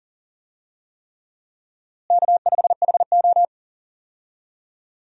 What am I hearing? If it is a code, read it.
Morse code, heard as a keyed tone, about 37 wpm, 700 Hz. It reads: K5HO
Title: USA Call Signs - 4 Characters - Repeated 3x - 40wpm